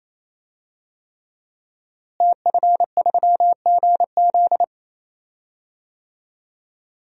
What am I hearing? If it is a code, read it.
Morse code, heard as a keyed tone, about 28 wpm, 700 Hz. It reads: TF3GZ
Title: DXCC-100 Call Signs - 5 Characters - Repeated 3x - 28wpm